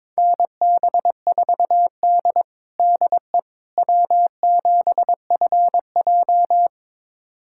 Morse code, heard as a keyed tone, about 22 wpm, 700 Hz. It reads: NB4D DE W7FJ
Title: USA Callsign DE Callsign - 4 Characters - 22wpm